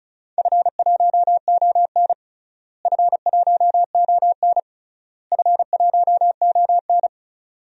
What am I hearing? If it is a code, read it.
Morse code, heard as a keyed tone, about 36 wpm, 700 Hz. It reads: F1ON F1ON F1ON
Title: DXCC-100 Call Signs - 4 Characters - Repeated 3x - 35wpm